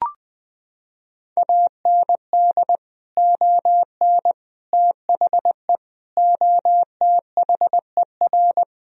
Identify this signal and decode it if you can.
Morse code, heard as a keyed tone, about 20 wpm, 700 Hz. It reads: AND ON THE OTHER
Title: Sentences from Top 1000 Words - Review Part 3 - 20wpm